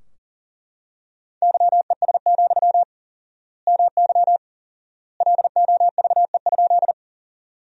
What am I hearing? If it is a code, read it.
Morse code, heard as a keyed tone, about 40 wpm, 700 Hz. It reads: YES, MY LOVE?
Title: 4x Word-Spacing - Sentences from Top 500 Words - 40wpm